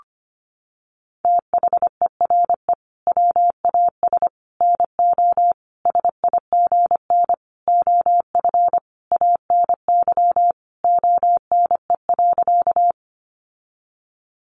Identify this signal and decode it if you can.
Morse code, heard as a keyed tone, about 25 wpm, 700 Hz. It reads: THERE WAS NO SIGN OF ANY ONE.
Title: Sentences from Top 1000 Words 25wpm